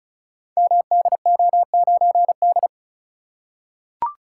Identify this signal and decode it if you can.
Morse code, heard as a keyed tone, about 36 wpm, 690 Hz. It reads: MDO9D
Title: Code Groups - 5 Letter:Numbers - 35wpm